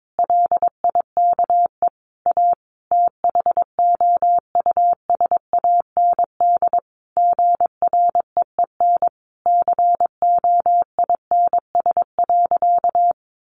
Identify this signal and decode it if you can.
Morse code, heard as a keyed tone, about 22 wpm, 700 Hz. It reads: LIKE A THOUSAND GREEN COINS.
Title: A Sound Of Thunder 22wpm